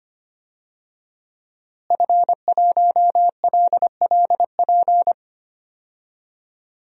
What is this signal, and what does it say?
Morse code, heard as a keyed tone, about 25 wpm, 700 Hz. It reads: F1LLP
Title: DXCC-100 Call Signs - 5 Characters - Repeated 2x - 25wpm